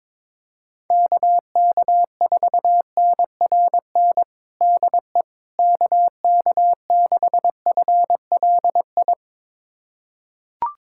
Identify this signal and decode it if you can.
Morse code, heard as a keyed tone, about 22 wpm, 700 Hz. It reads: KK4NRN DE KK6FLI
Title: USA Callsign DE Callsign - 6 Characters - 22wpm